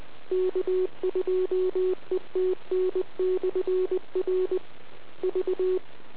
Morse code, heard as a keyed tone, about 20 wpm, 370 Hz. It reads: K2ETN/R V